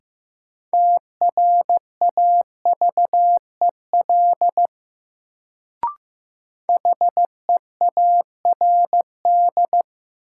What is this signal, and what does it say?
Morse code, heard as a keyed tone, about 15 wpm, 700 Hz. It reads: TRAVEL HEARD
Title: Sentences from Top 400 Words - New Word Drill - 15wpm